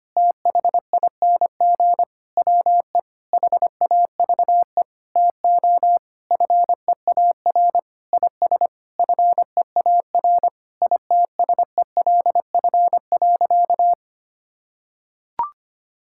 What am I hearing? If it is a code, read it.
Morse code, heard as a keyed tone, about 25 wpm, 700 Hz. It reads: THING WE HAVE TO FEAR IS FEAR ITSELF.
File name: practice / 25wpm / Sentences with 9-Word Phrases - 25wpm